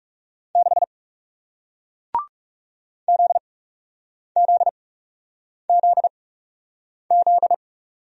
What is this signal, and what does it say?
Morse code, heard as a keyed tone, about 40 wpm, spaced as for 8 wpm, 700 Hz. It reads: 6 ZZZZ